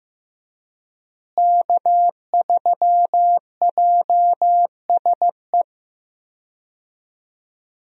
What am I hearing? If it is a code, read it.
Morse code, heard as a keyed tone, about 15 wpm, 700 Hz. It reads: K3JSE
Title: USA Call Signs - 5 Characters - Repeated 3x - 15wpm